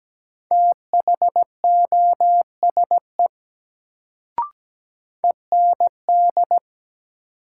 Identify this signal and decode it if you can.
Morse code, heard as a keyed tone, about 17 wpm, 700 Hz. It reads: THOSE END
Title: Sets of 2 Words - New Word Drill - 17wpm